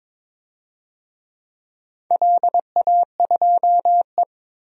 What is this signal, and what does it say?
Morse code, heard as a keyed tone, about 22 wpm, 700 Hz. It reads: LA2E
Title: DXCC-100 Call Signs - 4 Characters - Repeated 3x - 22wpm